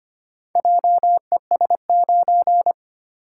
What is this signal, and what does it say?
Morse code, heard as a keyed tone, about 25 wpm, 700 Hz. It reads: JES9
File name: practice / 25wpm / Four Letter-Number Combinations - 25wpm